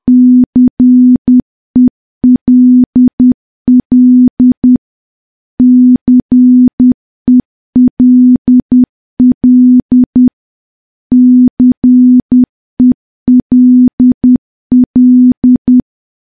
Morse code, heard as a keyed tone, about 10 wpm, 250 Hz. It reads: CELL CELL CELL
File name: morse / morse-training-10WPM